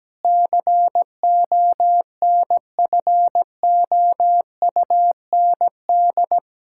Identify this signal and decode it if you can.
Morse code, heard as a keyed tone, about 17 wpm, 700 Hz. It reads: CONFOUND
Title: Pangram Challenge v2 - 17wpm